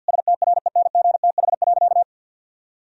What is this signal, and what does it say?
Morse code, heard as a keyed tone, about 50 wpm, 700 Hz. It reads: STRENGTH.
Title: Sentences from Top 1000 Words - Review Part 4 - 50wpm